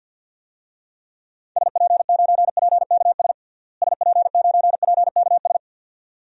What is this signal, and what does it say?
Morse code, heard as a keyed tone, about 47 wpm, 700 Hz. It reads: SP9PKS SP9PKS
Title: DXCC-100 Call Signs - 6 Characters - Repeated 2x - 50wpm